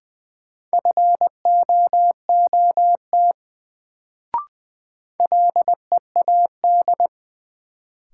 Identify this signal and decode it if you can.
Morse code, heard as a keyed tone, about 20 wpm, 700 Hz. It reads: FOOT LEAD
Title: Sentences from Top 500 Words - New Word Drill - 20wpm